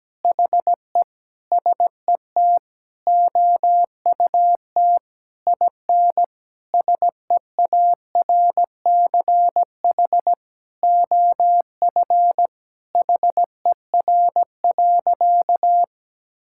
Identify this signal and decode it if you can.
Morse code, heard as a keyed tone, about 17 wpm, 700 Hz. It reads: HE SET OUT IN SEARCH OF HER.
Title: Sentences from Top 1000 Words - Review Part 5 - 17wpm